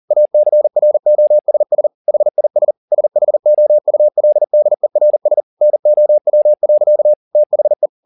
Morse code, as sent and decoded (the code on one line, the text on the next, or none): .- -.-. .-. --- ... ... / .... .. ... / ... .... --- ..- .-.. -.. .-. ... / -. --- .-- .-.-.- / - .... .
ACROSS HIS SHOULDERS NOW. THE